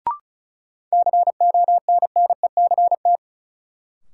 Morse code, heard as a keyed tone, about 36 wpm, 700 Hz. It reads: CONNECT
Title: Sentences from Top 1500 Words - New Word Drill Alt - 35wpm